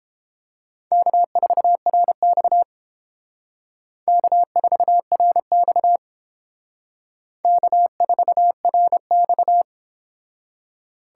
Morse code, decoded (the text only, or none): K4RX K4RX K4RX